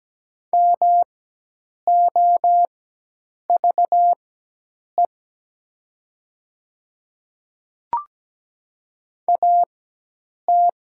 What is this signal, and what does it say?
Morse code, heard as a keyed tone, about 17 wpm, spaced as for 8 wpm, 700 Hz. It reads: MOVE AT
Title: 4x Character Spacing - Top 200 Words - 17wpm